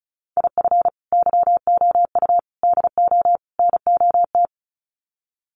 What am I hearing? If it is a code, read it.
Morse code, heard as a keyed tone, about 34 wpm, 700 Hz. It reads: IF YOU DO NOT
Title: Sets of 4 Words 35wpm